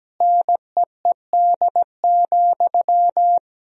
Morse code, heard as a keyed tone, about 17 wpm, 700 Hz. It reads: NEED,
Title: Sentences from Top 1000 Words - Review Part 1 - 17wpm